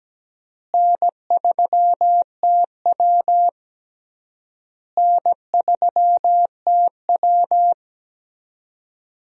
Morse code, heard as a keyed tone, about 17 wpm, 700 Hz. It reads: N3TW N3TW
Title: USA Call Signs - 4 Characters - Repeated 3x - 17wpm